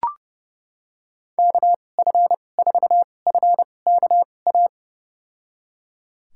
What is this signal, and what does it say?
Morse code, heard as a keyed tone, about 30 wpm, spaced as for 22 wpm, 700 Hz. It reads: KF4FKA
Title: USA Call Signs - 6 Characters - 2x Character Spacing - 30wpm